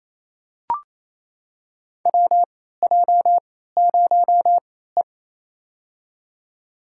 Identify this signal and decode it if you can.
Morse code, heard as a keyed tone, about 28 wpm, spaced as for 16 wpm, 700 Hz. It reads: WJ0E
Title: USA Call Signs - 4 Characters - 3x Character Spacing - 28wpm